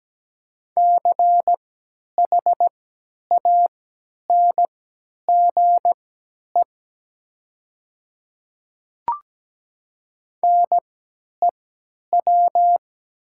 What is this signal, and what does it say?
Morse code, heard as a keyed tone, about 17 wpm, spaced as for 10 wpm, 700 Hz. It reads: CHANGE NEW
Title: 3x Character Spacing - Top 200 Words - 17wpm